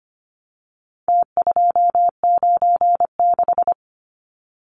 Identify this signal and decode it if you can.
Morse code, heard as a keyed tone, about 25 wpm, 700 Hz. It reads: T296
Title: Numbers - Contest Style - 4-Digit 25wpm